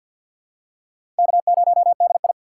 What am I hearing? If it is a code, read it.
Morse code, heard as a keyed tone, about 49 wpm, 700 Hz. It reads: K0DI
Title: USA Call Signs - 4x Character Spacing - 50wpm